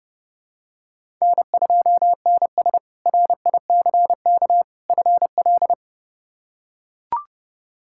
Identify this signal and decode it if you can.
Morse code, heard as a keyed tone, about 30 wpm, 700 Hz. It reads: N2NS RICK FL
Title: K1USN SST QSO Practice - 30wpm